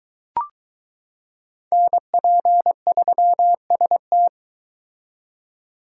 Morse code, heard as a keyed tone, about 22 wpm, 700 Hz. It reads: NP3ST